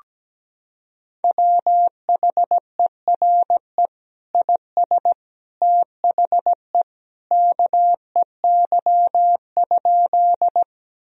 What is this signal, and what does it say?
Morse code, heard as a keyed tone, about 17 wpm, 700 Hz. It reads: WHERE IS THE KEY?
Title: Sentences from Top 1000 Words - Review Part 4 - 17wpm